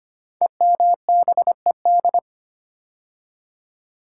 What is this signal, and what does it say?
Morse code, heard as a keyed tone, about 25 wpm, 700 Hz. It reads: EMBED
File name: 5 Letter Words - First 3 Letters Given - 25wpm